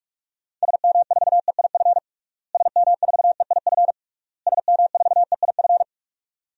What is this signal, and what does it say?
Morse code, heard as a keyed tone, about 46 wpm, 700 Hz. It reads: SM4EIF SM4EIF SM4EIF
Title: DXCC-100 Call Signs - 6 Characters - Repeated 3x - 45wpm